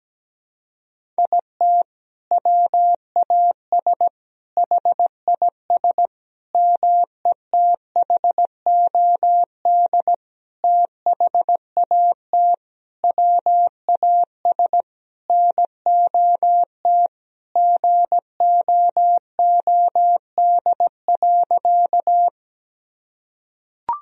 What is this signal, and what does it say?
Morse code, heard as a keyed tone, about 17 wpm, 700 Hz. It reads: IT WAS HIS METHOD THAT WAS NOT GOOD.
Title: Sentences from Top 1000 Words - Review Part 3 - 17wpm